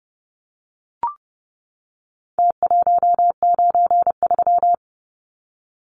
Morse code, heard as a keyed tone, about 30 wpm, 700 Hz. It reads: T193
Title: Numbers - Contest Style - 4-Digit 30wpm